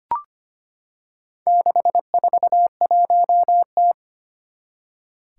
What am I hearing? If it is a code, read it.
Morse code, heard as a keyed tone, about 25 wpm, 700 Hz. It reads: 641T
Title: Four Challenging Letter-Number Combinations 25wpm